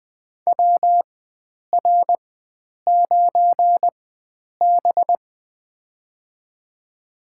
Morse code, heard as a keyed tone, about 20 wpm, spaced as for 9 wpm, 700 Hz. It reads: WR9B
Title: USA Call Signs - 4 Characters - 4x Character Spacing - 20wpm